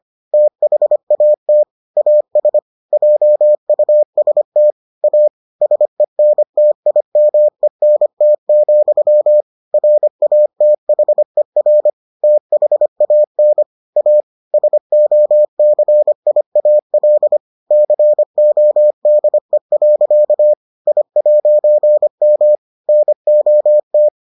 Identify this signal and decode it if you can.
Morse code, heard as a keyed tone, about 25 wpm, 600 Hz. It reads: THAT AS JUST A SENTIMENT, RATHER THAN A SOCIAL CODE. I'M NOT